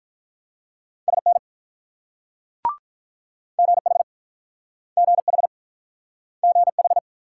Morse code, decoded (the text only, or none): SN GH GH GH